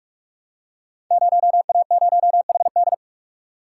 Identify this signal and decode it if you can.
Morse code, heard as a keyed tone, about 44 wpm, 700 Hz. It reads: 0A0HD
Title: Code Groups - 5 Letter:Numbers - 45wpm